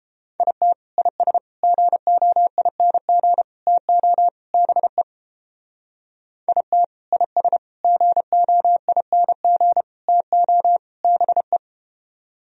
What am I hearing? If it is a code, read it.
Morse code, heard as a keyed tone, about 30 wpm, 700 Hz. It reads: IT IS GOING TO BE IT IS GOING TO BE